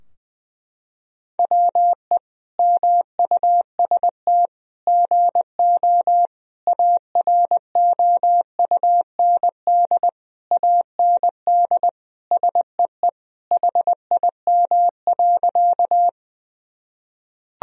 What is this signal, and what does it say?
Morse code, heard as a keyed tone, about 20 wpm, 700 Hz. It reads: WE MUST GO AROUND AND SEE HIM.
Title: Sentences from Top 1000 Words - Review Part 3 - 20wpm